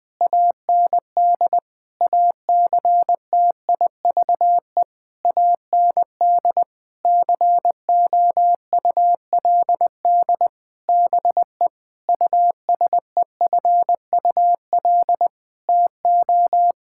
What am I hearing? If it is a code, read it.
Morse code, heard as a keyed tone, about 20 wpm, 700 Hz. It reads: AND ACTIVE AND COULD BE USEFUL TO
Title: Sentences from Top 1000 Words - Review Part 6 - 20wpm